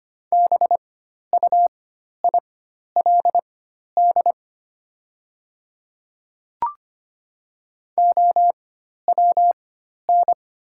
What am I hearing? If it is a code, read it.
Morse code, heard as a keyed tone, about 25 wpm, spaced as for 12 wpm, 700 Hz. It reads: BUILD OWN
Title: 4x Character Spacing - Top 500 Words - 25wpm